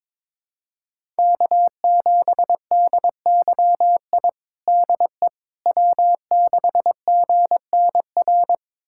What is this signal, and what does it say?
Morse code, heard as a keyed tone, about 22 wpm, 700 Hz. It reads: K7DYI DE W6GNR
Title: USA Callsign DE Callsign - 5 Characters - 22wpm